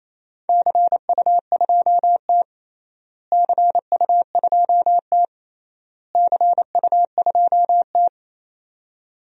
Morse code, heard as a keyed tone, about 28 wpm, 700 Hz. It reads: CU2T CU2T CU2T
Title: DXCC-100 Call Signs - 4 Characters - Repeated 3x - 28wpm